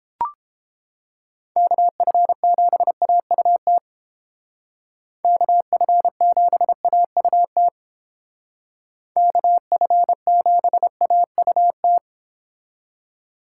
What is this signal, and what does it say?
Morse code, heard as a keyed tone, about 28 wpm, 700 Hz. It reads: KF7AUT KF7AUT KF7AUT